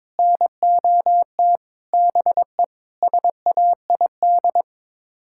Morse code, heard as a keyed tone, about 22 wpm, 700 Hz. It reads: NOT BE SAID
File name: practice / 22wpm / Sets of 6 Words - 22wpm